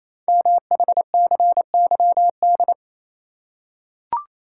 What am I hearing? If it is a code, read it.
Morse code, heard as a keyed tone, about 28 wpm, 700 Hz. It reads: MHCYD
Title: Code Groups - 5 Letters - 28wpm